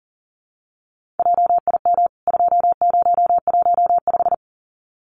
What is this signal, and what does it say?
Morse code, heard as a keyed tone, about 37 wpm, 700 Hz. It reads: JIM 2015
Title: CWT Exchange 40wpm